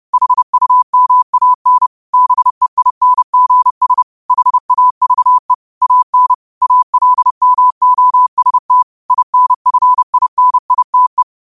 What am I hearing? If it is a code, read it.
Morse code, heard as a keyed tone, about 30 wpm, 1000 Hz. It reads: HUMAN BEINGS HAVE AN ALMOST INFINITE
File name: qotdfiles / 30wpm